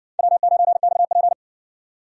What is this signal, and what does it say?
Morse code, heard as a keyed tone, about 60 wpm, 700 Hz. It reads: W9XP